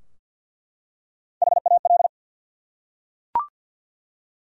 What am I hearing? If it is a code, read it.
Morse code, heard as a keyed tone, about 49 wpm, 700 Hz. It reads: HAL
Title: USA Common Names - Max 3 Letters - 50wpm